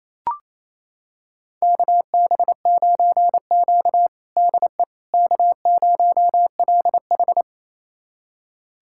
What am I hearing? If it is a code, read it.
Morse code, heard as a keyed tone, about 28 wpm, 700 Hz. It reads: KB9Q DE K0LH
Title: USA Callsign DE Callsign - 4 Characters - 28wpm